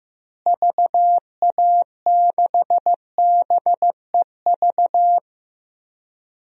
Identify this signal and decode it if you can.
Morse code, heard as a keyed tone, about 15 wpm, 700 Hz. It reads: VA6BEV
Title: DXCC-100 Call Signs - 6 Characters - Repeated 3x - 15wpm